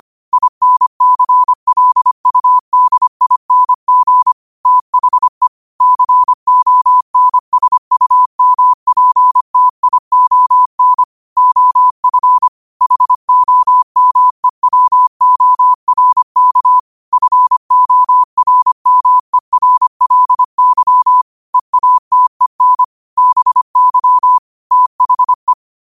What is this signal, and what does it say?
Morse code, heard as a keyed tone, about 25 wpm, 1000 Hz. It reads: INCLUDING THE CONSUMPTION OF HOMEWORK FORMERLY EATEN BY THE
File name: 25wpm